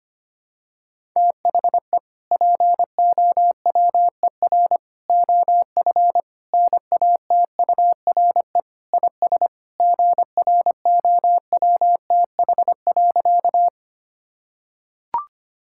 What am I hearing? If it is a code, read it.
Morse code, heard as a keyed tone, about 25 wpm, 700 Hz. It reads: THE POWER OF NATURE IS GROWTH.
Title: Sentences from Top 1000 Words - Review Part 5 - 25wpm